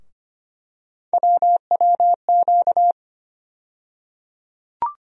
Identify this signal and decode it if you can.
Morse code, heard as a keyed tone, about 25 wpm, 700 Hz. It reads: WWQ
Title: USA Call Sign Postfixes - 3 Letters 25wpm